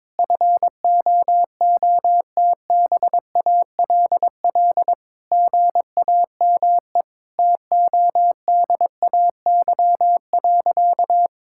Morse code, heard as a keyed tone, about 21 wpm, 700 Hz. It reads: FOOTBALL GAME TODAY.
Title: Sentences from Top 1600 Words 22wpm